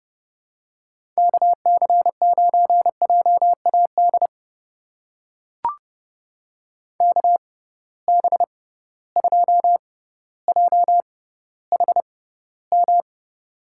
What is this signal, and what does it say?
Morse code, heard as a keyed tone, about 30 wpm, spaced as for 15 wpm, 700 Hz. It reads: KC9JAD KB2JHM
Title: USA Call Signs - 6 Characters - 6x Character Spacing - 30wpm